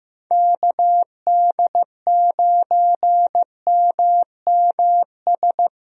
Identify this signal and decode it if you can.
Morse code, heard as a keyed tone, about 15 wpm, 700 Hz. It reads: KD9MMS
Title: USA Call Signs - 6 Characters - Repeated 3x - 15wpm